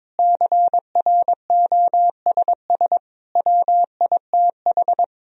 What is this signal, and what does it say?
Morse code, heard as a keyed tone, about 22 wpm, 700 Hz. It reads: CROSS WITH
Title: Sentences from Top 1000 Words - Review Part 1 - 22wpm